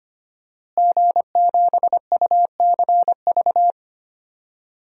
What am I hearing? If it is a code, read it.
Morse code, heard as a keyed tone, about 25 wpm, 700 Hz. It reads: G7UCV
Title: DXCC-100 Call Signs - 5 Characters - Repeated 2x - 25wpm